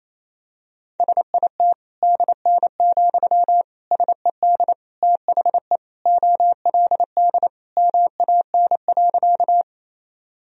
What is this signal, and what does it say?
Morse code, heard as a keyed tone, about 28 wpm, 700 Hz. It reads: SIT DN, SED THE OLD MAN.
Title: Sentences from Top 500 Words and Abbreviations 28wpm